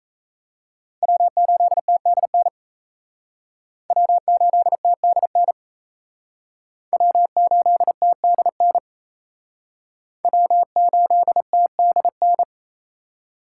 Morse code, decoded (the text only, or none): W8TDN W8TDN W8TDN W8TDN